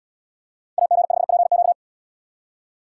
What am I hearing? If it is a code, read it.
Morse code, heard as a keyed tone, about 73 wpm, 690 Hz. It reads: AG5PQ